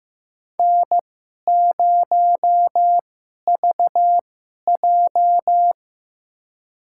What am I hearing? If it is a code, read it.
Morse code, heard as a keyed tone, about 15 wpm, spaced as for 11 wpm, 700 Hz. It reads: N0VJ
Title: USA Call Signs - 4 Characters - 2x Character Spacing - 15wpm